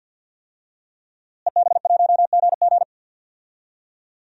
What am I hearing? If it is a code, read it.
Morse code, heard as a keyed tone, about 51 wpm, 700 Hz. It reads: EB1GG
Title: DXCC-100 Call Signs - 5 Characters - Repeated 2x - 50wpm